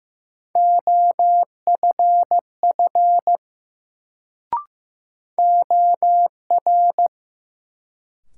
Morse code, heard as a keyed tone, about 15 wpm, 700 Hz. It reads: OFF OR
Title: Sets of 2 Words - New Word Drill - 15wpm